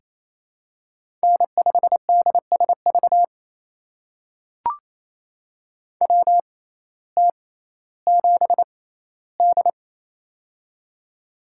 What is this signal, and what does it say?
Morse code, heard as a keyed tone, about 27 wpm, spaced as for 15 wpm, 700 Hz. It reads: N5DSV WT7D
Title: USA Call Signs - 6x Character Spacing - 28wpm